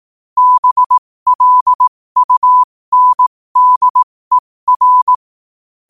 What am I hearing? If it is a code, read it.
Morse code, heard as a keyed tone, about 18 wpm, spaced as for 15 wpm, 1000 Hz. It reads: BLUNDER
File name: qotdfiles / 15wpm